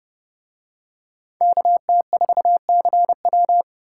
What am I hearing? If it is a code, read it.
Morse code, heard as a keyed tone, about 28 wpm, 700 Hz. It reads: KT4CW